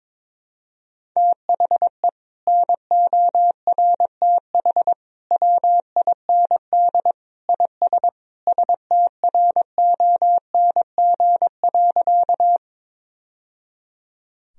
Morse code, heard as a keyed tone, about 22 wpm, 700 Hz. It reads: THE NORTH WIND IS STRONG.
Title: Sentences from Top 1000 Words - Review Part 2 - 22wpm